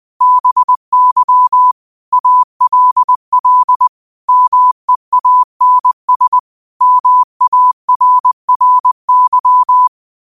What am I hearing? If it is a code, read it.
Morse code, heard as a keyed tone, about 20 wpm, 1000 Hz. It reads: BY ALL MEANS MARRY